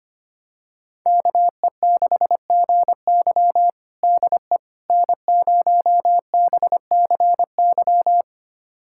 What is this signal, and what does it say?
Morse code, heard as a keyed tone, about 25 wpm, 700 Hz. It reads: KE6GY DE N0BCY
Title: USA Callsign DE Callsign 25wpm